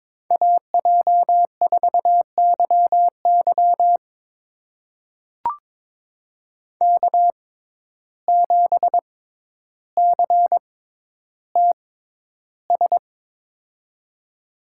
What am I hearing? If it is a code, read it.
Morse code, heard as a keyed tone, about 22 wpm, spaced as for 11 wpm, 700 Hz. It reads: AJ4YY K7CTS
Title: USA Call Signs - 6x Character Spacing - 22wpm